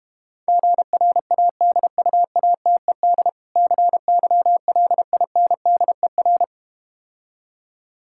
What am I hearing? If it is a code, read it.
Morse code, heard as a keyed tone, about 32 wpm, 700 Hz. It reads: GRADUATED CYLINDER